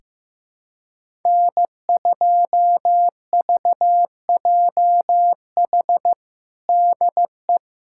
Morse code, heard as a keyed tone, about 16 wpm, 700 Hz. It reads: N2VJH DE
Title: USA Callsign DE Callsign 15wpm